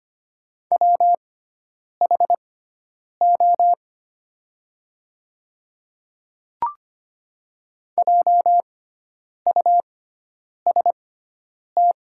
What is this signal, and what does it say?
Morse code, heard as a keyed tone, about 25 wpm, spaced as for 9 wpm, 700 Hz. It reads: WHO JUST
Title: 6x Character Spacing - Top 300 Words - 25wpm